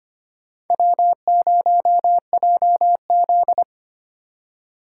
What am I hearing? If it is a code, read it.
Morse code, heard as a keyed tone, about 25 wpm, 690 Hz. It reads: W0JZ
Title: USA Call Signs - 4 Characters - 4x Character Spacing - 25wpm